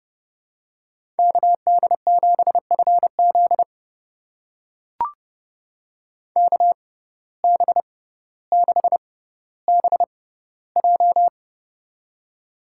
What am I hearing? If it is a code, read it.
Morse code, heard as a keyed tone, about 30 wpm, spaced as for 15 wpm, 700 Hz. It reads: KD7FZ KB6BJ
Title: USA Call Signs - 5 Characters - 6x Character Spacing - 30wpm